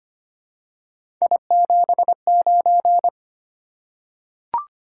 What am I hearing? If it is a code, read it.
Morse code, heard as a keyed tone, about 25 wpm, 700 Hz. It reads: I79
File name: Three Letter-Number Combinations - 25wpm